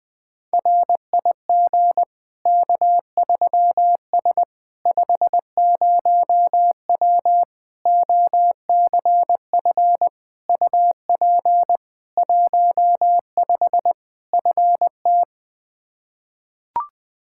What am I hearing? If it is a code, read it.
Morse code, heard as a keyed tone, about 20 wpm, 700 Hz. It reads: RIG K3S 50W OCF UP 15 FT
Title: Practice QSO - Exchange 2 - Info - 20wpm